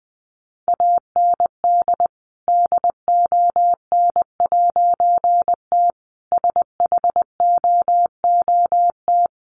Morse code, heard as a keyed tone, about 20 wpm, 700 Hz. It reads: AND DON'T SHOOT
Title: A Sound Of Thunder 20wpm